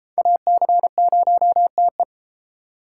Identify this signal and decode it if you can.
Morse code, heard as a keyed tone, about 31 wpm, 700 Hz. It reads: AC0TE